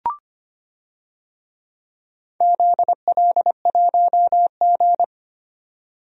Morse code, heard as a keyed tone, about 25 wpm, 700 Hz. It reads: ZL1G
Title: DXCC-100 Call Signs - 4 Characters - 25wpm